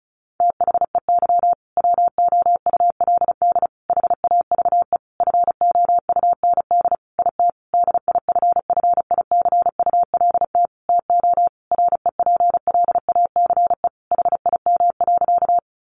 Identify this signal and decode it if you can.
Morse code, heard as a keyed tone, about 34 wpm, 700 Hz. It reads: THEY WOULD HAVE FOUND IT DIFFICULT TO REPLACE HIM.
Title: Sentences from Top 1400 Words 35wpm